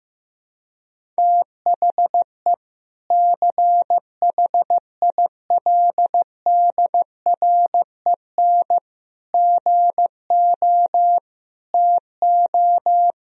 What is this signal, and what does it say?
Morse code, heard as a keyed tone, about 15 wpm, 700 Hz. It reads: THE CHILDREN GO TO
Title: Sentences from Top 1000 Words - Review Part 1 - 15wpm